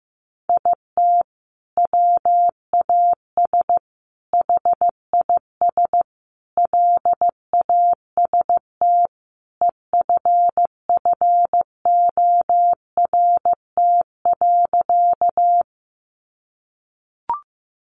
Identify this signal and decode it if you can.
Morse code, heard as a keyed tone, about 14 wpm, 700 Hz. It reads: IT WAS HIS LAST EFFORT.
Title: Sentences from Top 1200 Words 15wpm